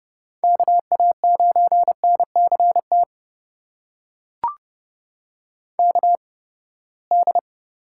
Morse code, heard as a keyed tone, about 30 wpm, spaced as for 21 wpm, 700 Hz. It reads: KA9NCT KD
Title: USA Call Signs - 8x Character Spacing - 30wpm